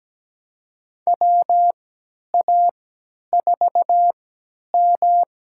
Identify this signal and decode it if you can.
Morse code, heard as a keyed tone, about 17 wpm, spaced as for 10 wpm, 700 Hz. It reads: WA4M
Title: USA Call Signs - 3x Character Spacing - 17wpm